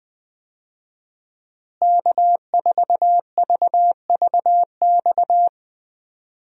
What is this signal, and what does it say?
Morse code, heard as a keyed tone, about 20 wpm, 700 Hz. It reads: K4VVX
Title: USA Call Signs - 5 Characters - Repeated 3x - 20wpm